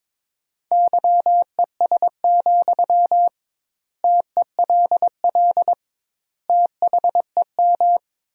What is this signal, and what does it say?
Morse code, heard as a keyed tone, about 22 wpm, 700 Hz. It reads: YES, TELL THEM
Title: 2x Word-Spacing - Sentences from Top 500 Words - 22wpm